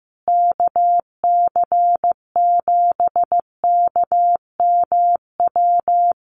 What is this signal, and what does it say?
Morse code, heard as a keyed tone, about 15 wpm, 700 Hz. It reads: KC7KMW
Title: USA Call Signs Repeated 3x 15wpm